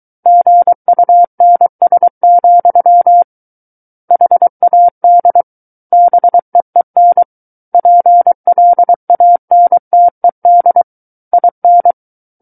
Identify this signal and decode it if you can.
Morse code, heard as a keyed tone, about 23 wpm, 700 Hz. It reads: GUNS, HAD BEEN PLANTED IN
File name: Book One / 14-Chapter 14 - 23 WPM